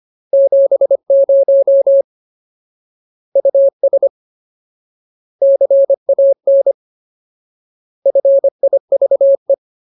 Morse code, heard as a keyed tone, about 25 wpm, 550 Hz. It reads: 70 US CAN FIVE
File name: Ultimate Word List / 25WPM Ultimate